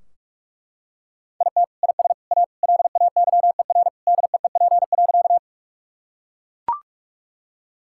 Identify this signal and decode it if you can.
Morse code, heard as a keyed tone, about 43 wpm, 700 Hz. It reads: IT IS A LAYER DEEP.
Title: Sentences from Top 1600 Words 45wpm